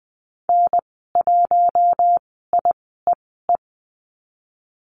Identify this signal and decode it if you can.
Morse code, heard as a keyed tone, about 20 wpm, spaced as for 14 wpm, 700 Hz. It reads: N1IEE
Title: 055 Lesson 10g - Letter r - Call Signs 20wpm